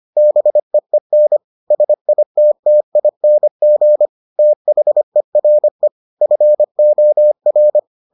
Morse code, heard as a keyed tone, about 25 wpm, 600 Hz. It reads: BEEN SITTING THERE FOR